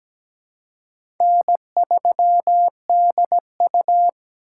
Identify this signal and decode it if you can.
Morse code, heard as a keyed tone, about 17 wpm, 700 Hz. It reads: N3DU
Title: USA Call Signs - 4 Characters - 3x Character Spacing - 17wpm